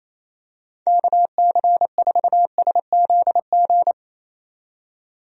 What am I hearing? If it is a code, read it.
Morse code, heard as a keyed tone, about 28 wpm, 700 Hz. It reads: KC4SZG
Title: USA Call Signs - 6 Characters - 28wpm